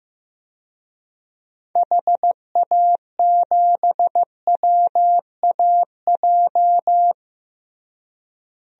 Morse code, heard as a keyed tone, about 15 wpm, 700 Hz. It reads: HA7WAJ
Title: DXCC-100 Call Signs - 6 Characters - Repeated 2x - 15wpm